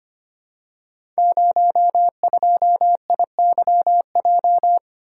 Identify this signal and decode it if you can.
Morse code, heard as a keyed tone, about 25 wpm, 700 Hz. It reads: 02IYJ
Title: Code Groups - 5 Letter:Numbers - 25wpm